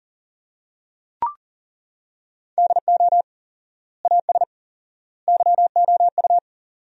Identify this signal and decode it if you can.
Morse code, heard as a keyed tone, about 40 wpm, 700 Hz. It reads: DO AS YOU
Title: 4x Word-Spacing - Sentences from Top 100 Words - 40wpm